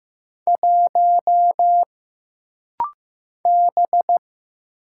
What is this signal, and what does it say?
Morse code, heard as a keyed tone, about 15 wpm, 700 Hz. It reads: 1 B
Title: Single Letter-Number Alt - Rapid-Fire 15wpm